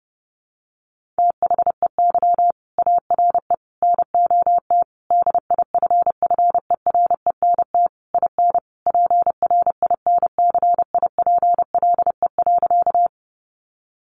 Morse code, heard as a keyed tone, about 28 wpm, 700 Hz. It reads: THEY ARE NOT DIFFERENT IN PRINCIPLE.
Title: Sentences from Top 1400 Words 30wpm